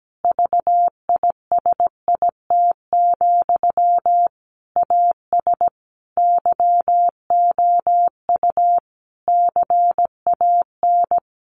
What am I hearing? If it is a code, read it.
Morse code, heard as a keyed tone, about 17 wpm, 700 Hz. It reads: VISIT, AS YOU CAN
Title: Sentences from Top 1400 Words 17wpm